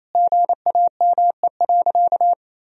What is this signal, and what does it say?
Morse code, heard as a keyed tone, about 28 wpm, 700 Hz. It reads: GAME.
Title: Sentences from Top 1000 Words - Review Part 4 - 28wpm